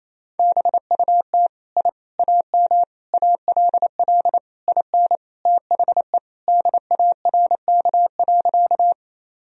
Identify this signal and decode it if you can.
Morse code, heard as a keyed tone, about 28 wpm, 700 Hz. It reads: BUT I AM ALL IN THE DARK.
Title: Sentences from Top 1000 Words - Review Part 2 - 28wpm